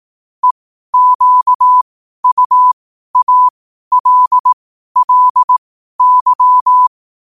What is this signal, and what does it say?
Morse code, heard as a keyed tone, about 18 wpm, spaced as for 12 wpm, 1000 Hz. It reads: EQUALLY